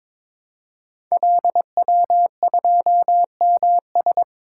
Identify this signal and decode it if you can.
Morse code, heard as a keyed tone, about 22 wpm, 700 Hz. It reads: LW2MS
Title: Code Groups - 5 Letter:Numbers - 22wpm